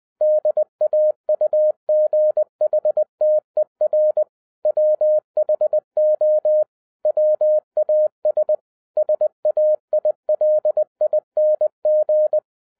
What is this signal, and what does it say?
Morse code, heard as a keyed tone, about 20 wpm, 610 Hz. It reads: DAUGHTER WHO WAS SAILING